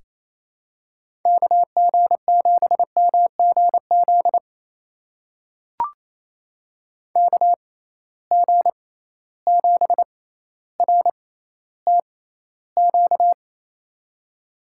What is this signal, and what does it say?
Morse code, heard as a keyed tone, about 28 wpm, spaced as for 14 wpm, 700 Hz. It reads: KG7MGZ KG7RTQ
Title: USA Call Signs - 6x Character Spacing - 28wpm